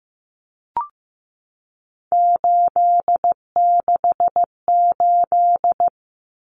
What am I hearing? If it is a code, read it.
Morse code, heard as a keyed tone, about 15 wpm, 700 Hz. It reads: 868
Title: Numbers - 3-Digit 15wpm